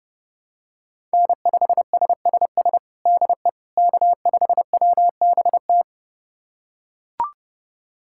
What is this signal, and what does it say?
Morse code, heard as a keyed tone, about 30 wpm, 700 Hz. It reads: N5SSS DE K5WBT
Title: USA Callsign DE Callsign - 5 Characters - 30wpm